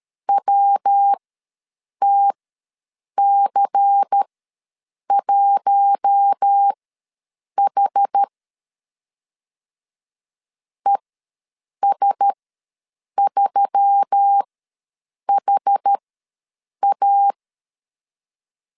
Morse code, heard as a keyed tone, about 13 wpm, spaced as for 7 wpm, 800 Hz. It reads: WTC1H ES3HA
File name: LEZIONI / 5 LBVC / 5 LBVC ws 12wpm 3159